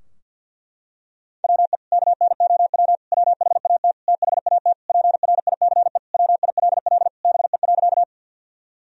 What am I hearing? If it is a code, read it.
Morse code, heard as a keyed tone, about 47 wpm, 700 Hz. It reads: WE KNOW WHAT THAT PRICE WILL BE.
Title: Sentences from Top 1000 Words - Review Part 4 - 50wpm